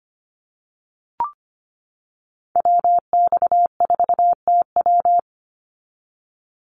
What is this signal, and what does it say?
Morse code, heard as a keyed tone, about 25 wpm, 700 Hz. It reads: WX4TW
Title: NA SOTA Activator Callsigns 25wpm